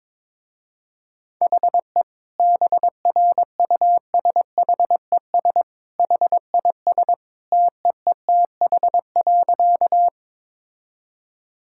Morse code, heard as a keyed tone, about 22 wpm, 700 Hz. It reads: HE BRUSHES HIS TEETH.
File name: practice / 22wpm / Easy Sentences with Hints - 22wpm